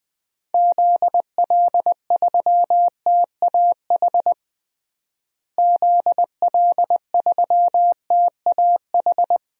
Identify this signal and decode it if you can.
Morse code, heard as a keyed tone, about 20 wpm, 700 Hz. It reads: ZL3TAH ZL3TAH
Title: DXCC-100 Call Signs - 6 Characters - Repeated 3x - 20wpm